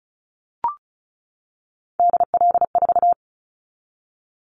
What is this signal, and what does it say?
Morse code, heard as a keyed tone, about 36 wpm, 700 Hz. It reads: DL4
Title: Callsign Prefixes for Top 25 DXCC - 35wpm